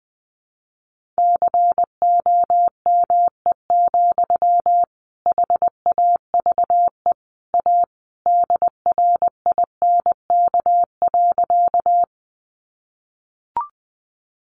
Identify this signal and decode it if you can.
Morse code, heard as a keyed tone, about 19 wpm, 700 Hz. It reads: COME, HAVE A DRINK.
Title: Sentences from Top 1500 Words 20wpm